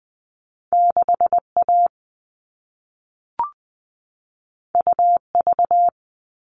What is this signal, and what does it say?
Morse code, heard as a keyed tone, about 20 wpm, 700 Hz. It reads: 6A UV
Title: All Cut Numbers - 2-Digit 20wpm